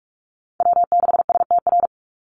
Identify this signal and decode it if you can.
Morse code, heard as a keyed tone, about 44 wpm, 700 Hz. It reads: W6STR
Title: NA SOTA Chaser Callsigns 45wpm